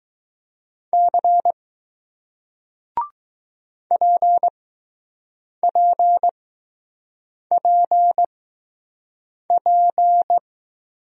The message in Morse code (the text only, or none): C PPPP